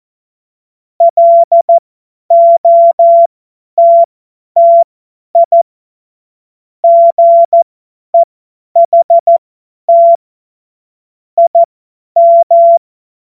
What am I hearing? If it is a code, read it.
Morse code, heard as a keyed tone, about 14 wpm, spaced as for 10 wpm, 680 Hz. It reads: LOTTI GEHT IM
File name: Morsekurs / Witze / Witz 20. Januar 2026 50 F70